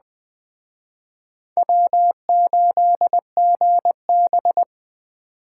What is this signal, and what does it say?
Morse code, heard as a keyed tone, about 21 wpm, 700 Hz. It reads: W8GB
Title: Speed Racing - USA 1x2 Call Signs 17wpm